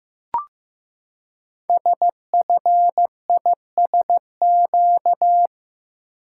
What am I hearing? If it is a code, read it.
Morse code, heard as a keyed tone, about 15 wpm, 700 Hz. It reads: SFISQ
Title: Code Groups - 5 Letter:Numbers - 15wpm